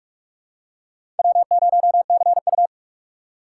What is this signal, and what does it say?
Morse code, heard as a keyed tone, about 47 wpm, 700 Hz. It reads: W0CU
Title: USA Call Signs - 4x Character Spacing - 45wpm